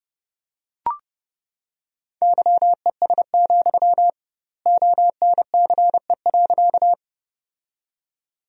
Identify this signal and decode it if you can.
Morse code, heard as a keyed tone, about 30 wpm, 700 Hz. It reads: YES, ONCE.
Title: 2x Word-Spacing - Sentences from Top 500 Words - 30wpm